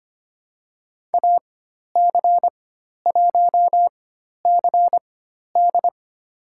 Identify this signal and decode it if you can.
Morse code, heard as a keyed tone, about 25 wpm, spaced as for 12 wpm, 700 Hz. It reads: AC1CD
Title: USA Call Signs - 5 Characters - 4x Character Spacing - 25wpm